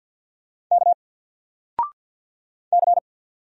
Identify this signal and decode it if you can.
Morse code, heard as a keyed tone, about 48 wpm, 700 Hz. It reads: K C
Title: Single Letter-Number 50wpm